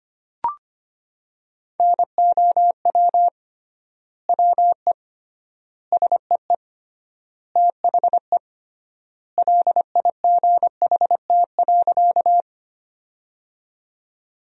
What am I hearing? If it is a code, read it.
Morse code, heard as a keyed tone, about 25 wpm, 700 Hz. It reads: NOW WE SEE THE LIGHT.
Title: 3x Word-Spacing - Sentences from Top 200 Words - 25wpm